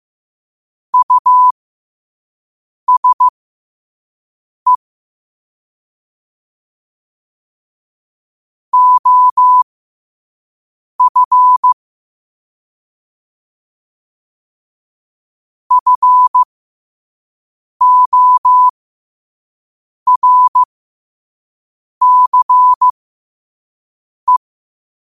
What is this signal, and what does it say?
Morse code, heard as a keyed tone, about 15 wpm, spaced as for 5 wpm, 1000 Hz. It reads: USE OF FORCE
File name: qotdfiles / 5wpm